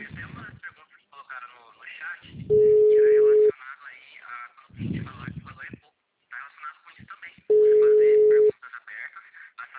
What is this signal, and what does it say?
Ringback tone (Europe, 425 Hz), with speech.